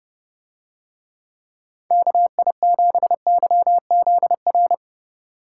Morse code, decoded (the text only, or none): KI7YZR